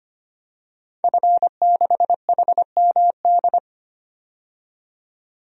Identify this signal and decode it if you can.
Morse code, heard as a keyed tone, about 25 wpm, 700 Hz. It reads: F6HMD
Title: DXCC-100 Call Signs - 5 Characters - 25wpm